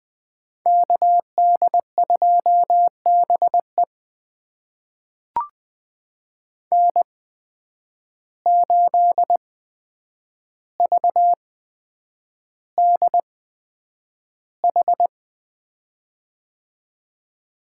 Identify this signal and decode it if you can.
Morse code, heard as a keyed tone, about 20 wpm, spaced as for 9 wpm, 700 Hz. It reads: KD2BE N8VDH
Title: USA Call Signs - 5 Characters - 8x Character Spacing - 20wpm